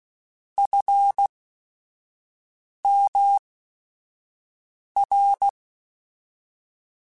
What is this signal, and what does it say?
Morse code, heard as a keyed tone, about 16 wpm, 800 Hz. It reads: F M R
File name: Technical Subjects / Technical Subjects-mp3s / Sw Pwr Sup @5